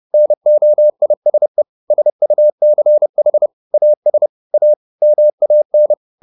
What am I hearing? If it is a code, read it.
Morse code, heard as a keyed tone, about 30 wpm, 600 Hz. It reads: NOISE SUCH AS A MAN